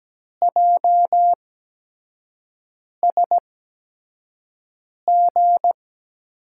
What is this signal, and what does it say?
Morse code, heard as a keyed tone, about 17 wpm, spaced as for 5 wpm, 700 Hz. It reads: JSG
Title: USA Call Signs - 8x Character Spacing - 17wpm